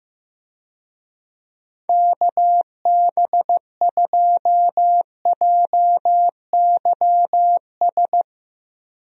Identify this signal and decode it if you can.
Morse code, heard as a keyed tone, about 15 wpm, 700 Hz. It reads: KB2JYS
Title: USA Call Signs - 6 Characters - 15wpm